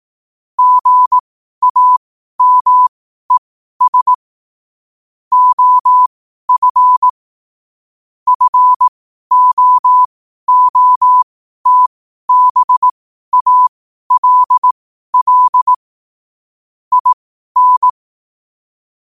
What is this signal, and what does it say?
Morse code, heard as a keyed tone, about 18 wpm, spaced as for 12 wpm, 1000 Hz. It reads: GAMES OF FOOTBALL IN
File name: qotdfiles / 12wpm